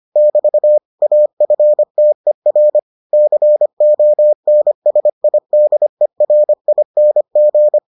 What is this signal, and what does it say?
Morse code, heard as a keyed tone, about 25 wpm, 600 Hz. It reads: <BT> AFTER CONSIDERING